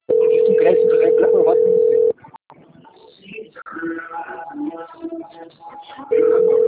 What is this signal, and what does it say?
Ringback tone (North America), 440 + 480 Hz, with speech.
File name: out- - - - - .22075